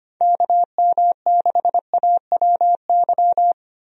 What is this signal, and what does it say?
Morse code, heard as a keyed tone, about 25 wpm, 700 Hz. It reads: KM6AWY
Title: USA Call Signs - 6 Characters - 25wpm